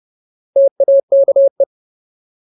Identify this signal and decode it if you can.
Morse code, heard as a keyed tone, about 30 wpm, 550 Hz. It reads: TAKE
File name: Ultimate Word List / 30WPM Ultimate